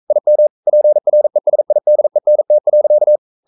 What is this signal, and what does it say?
Morse code, heard as a keyed tone, about 42 wpm, 600 Hz. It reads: IM PRESIDENT.
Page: Romanen Never af Ken Follet på engelsk.